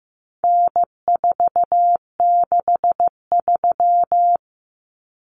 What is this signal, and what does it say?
Morse code, heard as a keyed tone, about 15 wpm, 700 Hz. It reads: N463
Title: Common Cut Numbers - 4-Digit 15wpm